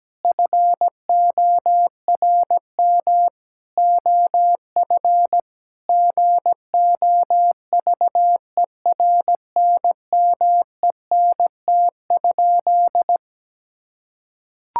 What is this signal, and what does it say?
Morse code, heard as a keyed tone, about 17 wpm, 700 Hz. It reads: FORM OF GOVERNMENT?
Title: Sentences from Top 1000 Words - Review Part 6 - 17wpm